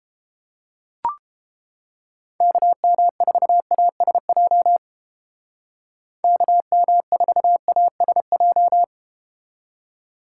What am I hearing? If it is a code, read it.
Morse code, heard as a keyed tone, about 30 wpm, 700 Hz. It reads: KM4ASJ KM4ASJ